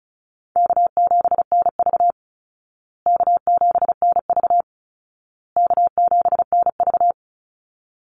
Morse code, heard as a keyed tone, about 34 wpm, 700 Hz. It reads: K7NV K7NV K7NV
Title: Easy Call Signs Repeated 3x - 35wpm